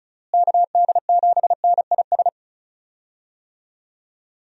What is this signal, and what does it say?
Morse code, heard as a keyed tone, about 36 wpm, 700 Hz. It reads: KD7NIS
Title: USA Call Signs - 6 Characters - Repeated 2x - 35wpm